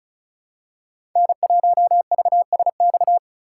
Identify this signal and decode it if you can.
Morse code, heard as a keyed tone, about 36 wpm, 700 Hz. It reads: N1VSX
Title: USA Call Signs - 5 Characters - 2x Character Spacing - 35wpm